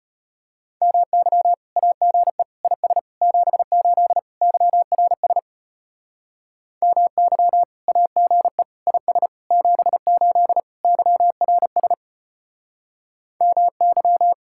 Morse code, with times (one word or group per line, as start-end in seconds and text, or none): 0.8-1.5 MY
1.8-2.4 AGE
2.6-3.0 IS
3.2-4.2 78
4.4-5.4 YRS
6.8-7.6 MY
7.9-8.6 AGE
8.9-9.3 IS
9.5-10.6 78
10.8-11.9 YRS
13.4-14.3 MY